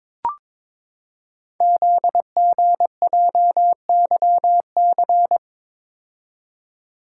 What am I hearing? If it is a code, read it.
Morse code, heard as a keyed tone, about 22 wpm, 700 Hz. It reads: ZGJYC
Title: Code Groups - 5 Letter:Numbers - 22wpm